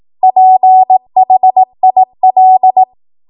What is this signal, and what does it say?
Morse code, heard as a keyed tone, about 18 wpm, 750 Hz. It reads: PHIL